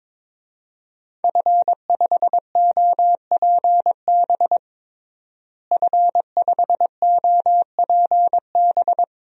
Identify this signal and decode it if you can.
Morse code, heard as a keyed tone, about 22 wpm, 700 Hz. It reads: F5OPB F5OPB
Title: DXCC-100 Call Signs - 5 Characters - Repeated 2x - 22wpm